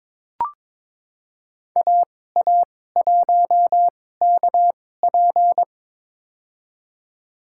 Morse code, heard as a keyed tone, about 23 wpm, spaced as for 16 wpm, 700 Hz. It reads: AA1KP
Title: USA Call Signs - 5 Characters - 2x Character Spacing - 22wpm